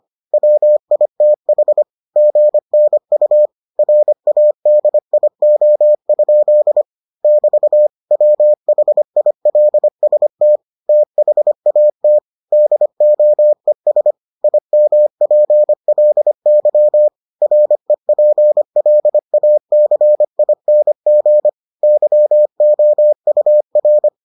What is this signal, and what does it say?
Morse code, heard as a keyed tone, about 25 wpm, 600 Hz. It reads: WITH GNU RADIO? <BT> WHILST THAT DOES IMPLY REPLACING YOUR